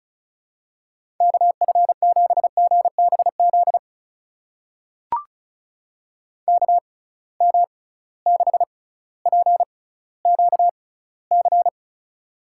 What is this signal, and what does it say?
Morse code, heard as a keyed tone, about 34 wpm, spaced as for 18 wpm, 700 Hz. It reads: KF7GBZ KM6PQC
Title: USA Call Signs - 6 Characters - 6x Character Spacing - 35wpm